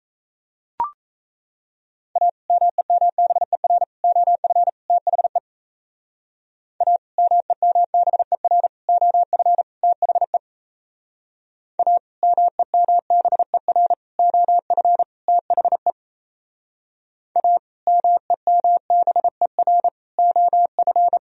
Morse code, heard as a keyed tone, about 34 wpm, 700 Hz. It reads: A MEMBER OF THE A MEMBER OF THE A MEMBER OF THE A MEMBER OF
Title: Speed-Racing - Sets of 4 Words 28wpm